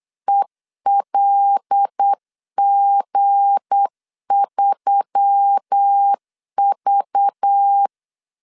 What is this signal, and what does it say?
Morse code, heard as a keyed tone, about 8 wpm, 800 Hz. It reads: ELG3V
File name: LEZIONI / 10 SIGNS / 10D COMA 8wpm 3749